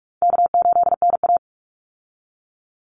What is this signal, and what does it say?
Morse code, heard as a keyed tone, about 44 wpm, 700 Hz. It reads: K8NA
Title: Easy Call Signs - 45wpm